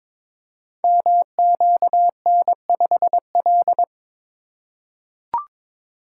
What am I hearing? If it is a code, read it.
Morse code, heard as a keyed tone, about 22 wpm, 700 Hz. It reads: MQN5L
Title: Code Groups - 5 Letter:Numbers - 22wpm